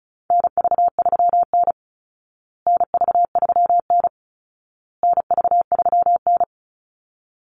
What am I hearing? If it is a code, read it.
Morse code, heard as a keyed tone, about 34 wpm, 700 Hz. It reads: NV3N NV3N NV3N
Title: Call Signs Repeated 2x - 35wpm